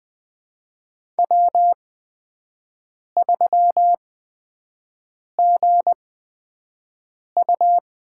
Morse code, heard as a keyed tone, about 20 wpm, spaced as for 5 wpm, 700 Hz. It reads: W3GU
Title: USA Call Signs - 4 Characters - 8x Character Spacing - 20wpm